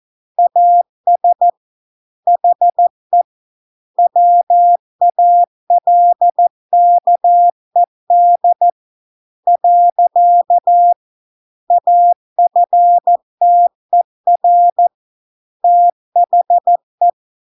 Morse code, heard as a keyed tone, about 14 wpm, 700 Hz. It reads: AS HE WALKED . AFTER THE